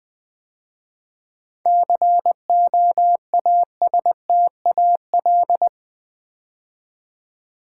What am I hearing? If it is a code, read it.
Morse code, heard as a keyed tone, about 20 wpm, 700 Hz. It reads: COASTAL
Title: 7 Letter Words - First 3 Letters Given - 20wpm